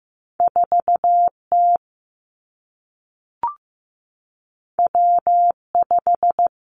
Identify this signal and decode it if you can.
Morse code, heard as a keyed tone, about 15 wpm, 700 Hz. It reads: 4 T W 5
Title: Two Challenging Letter-Number Combinations 15wpm